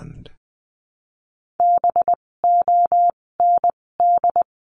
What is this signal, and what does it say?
Morse code, heard as a keyed tone, about 20 wpm, spaced as for 16 wpm, 700 Hz. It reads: BOND
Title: 256 Lesson 42a - Words 12-20wpm